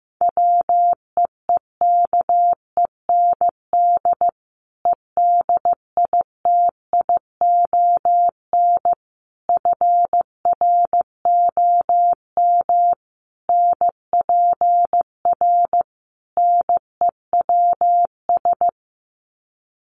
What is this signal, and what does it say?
Morse code, heard as a keyed tone, about 15 wpm, 700 Hz. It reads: WEEKEND EDITION FROM NPR NEWS
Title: Sets of 5 Words 15wpm